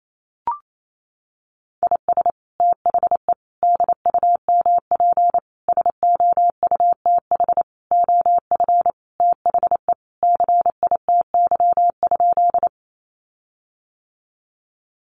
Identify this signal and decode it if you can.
Morse code, heard as a keyed tone, about 28 wpm, 700 Hz. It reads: IS THE DUMP SOUTH OF THE CITY?
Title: Sentences from Top 1300 Words 28wpm